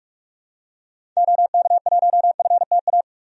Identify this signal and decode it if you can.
Morse code, heard as a keyed tone, about 42 wpm, 700 Hz. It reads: OK1FTA
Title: DXCC-100 Call Signs - 6 Characters - 45wpm